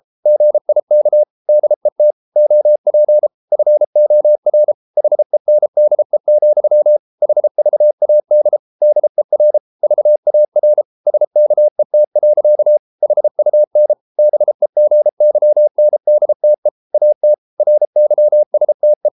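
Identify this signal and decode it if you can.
Morse code, heard as a keyed tone, about 33 wpm, 600 Hz. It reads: GIK DET OP FOR HENDE, HVAD DER VAR SKET. HUN BEGYNDTE AT RYSTE
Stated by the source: Morse code at 33 wpm, QRQ.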